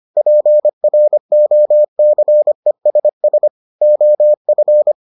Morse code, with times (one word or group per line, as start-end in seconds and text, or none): 0.2-3.5 PROCESS
3.8-4.9 OF